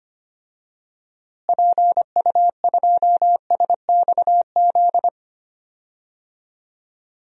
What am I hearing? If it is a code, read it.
Morse code, heard as a keyed tone, about 25 wpm, 700 Hz. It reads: PU2SXZ
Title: DXCC-100 Call Signs - 6 Characters - Repeated 2x - 25wpm